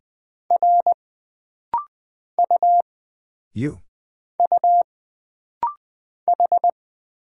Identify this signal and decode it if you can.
Morse code, heard as a keyed tone, about 20 wpm, 700 Hz. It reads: R U U H
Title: Single Character (Letter:Number:Punctuation:Prosign) - Rapid-Fire - 20wpm